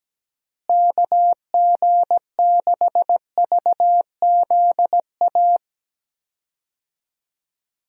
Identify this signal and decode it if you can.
Morse code, heard as a keyed tone, about 17 wpm, 700 Hz. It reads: KG6VZA
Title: USA Call Signs - 6 Characters - Repeated 3x - 17wpm